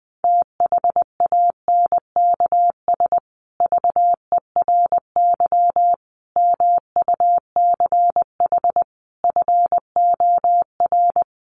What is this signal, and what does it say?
Morse code, heard as a keyed tone, about 20 wpm, 700 Hz. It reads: THANKS VERY MUCH FOR
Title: Sets of 5 Words Encore2 20wpm